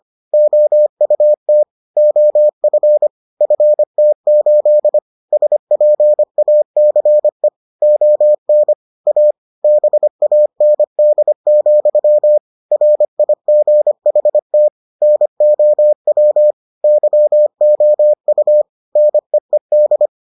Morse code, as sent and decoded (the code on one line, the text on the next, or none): --- ..- - / --- ..-. / ..-. - ---.. / ... .--. .- -.-. / --- -. / .- / -... .- -. -.. --..-- / .-. .. --. .... - / -. --- .-- / -.-- --- ..- / -. . -..
OUT OF FT8 SPACE ON A BAND, RIGHT NOW YOU NEED